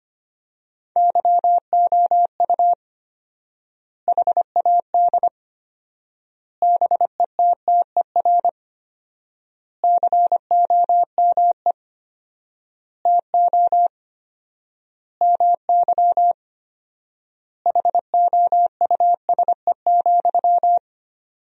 Morse code, heard as a keyed tone, about 25 wpm, 700 Hz. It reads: YOU HAD BETTER COME TO MY HOUSE,
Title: 4x Word-Spacing - Sentences from Top 400 Words - 25wpm